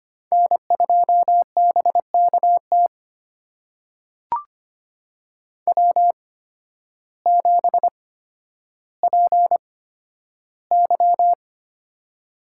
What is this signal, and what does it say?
Morse code, heard as a keyed tone, about 25 wpm, spaced as for 12 wpm, 700 Hz. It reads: N2BKT W7PY
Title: USA Call Signs - 8x Character Spacing - 25wpm